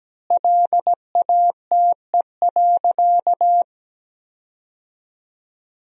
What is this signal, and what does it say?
Morse code, heard as a keyed tone, about 17 wpm, 700 Hz. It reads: LATE.
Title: 4x Word-Spacing - Sentences from Top 300 Words - 17wpm